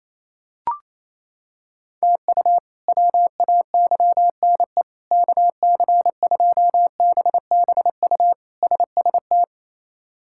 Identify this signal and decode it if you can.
Morse code, heard as a keyed tone, about 28 wpm, 700 Hz. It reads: TU WAYNE KC2BBU SST
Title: K1USN SST QSO Practice - 28wpm